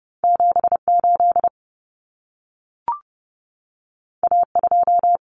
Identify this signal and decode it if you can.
Morse code, heard as a keyed tone, about 30 wpm, 700 Hz. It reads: 78 A2
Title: Common Cut Numbers - 2-Digit 30wpm